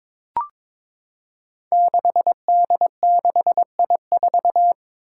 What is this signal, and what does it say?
Morse code, heard as a keyed tone, about 22 wpm, 700 Hz. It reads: 6D6I4
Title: Five Challenging Letter-Number Combinations 22wpm